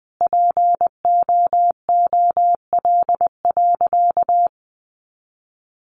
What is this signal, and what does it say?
Morse code, heard as a keyed tone, about 20 wpm, 700 Hz. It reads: POOL.
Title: Additional Easy Phrases 20wpm